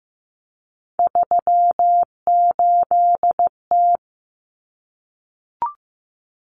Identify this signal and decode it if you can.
Morse code, heard as a keyed tone, about 15 wpm, 700 Hz. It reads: 38T
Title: Common Cut Numbers - 3-Digit 15wpm